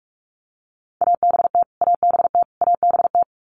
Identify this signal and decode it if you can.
Morse code, heard as a keyed tone, about 46 wpm, 700 Hz. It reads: ABT ABT ABT
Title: Abbreviations Repeated 3x - 45wpm